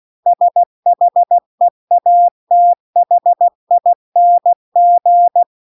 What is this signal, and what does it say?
Morse code, heard as a keyed tone, about 16 wpm, 700 Hz. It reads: SHEATHING